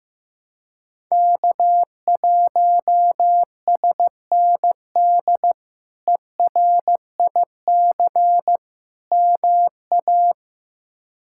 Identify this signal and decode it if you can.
Morse code, heard as a keyed tone, about 15 wpm, 700 Hz. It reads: K1SND ERIC MA
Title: K1USN SST QSO Practice - 15wpm